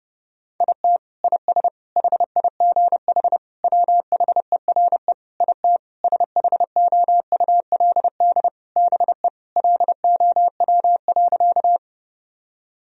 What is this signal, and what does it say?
Morse code, heard as a keyed tone, about 30 wpm, 700 Hz. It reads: IT IS HIGH WHERE IT SHOULD BE LOW.
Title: Sentences from Top 1000 Words - Review Part 1 - 30wpm